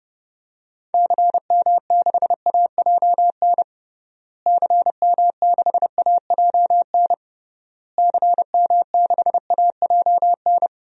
Morse code, heard as a keyed tone, about 32 wpm, 700 Hz. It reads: CM6AJN CM6AJN CM6AJN
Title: DXCC-100 Call Signs - 6 Characters - Repeated 3x - 30wpm